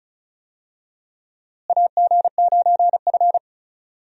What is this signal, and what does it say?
Morse code, heard as a keyed tone, about 34 wpm, 700 Hz. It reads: AG9F
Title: USA Call Signs - 4 Characters - Repeated 3x - 35wpm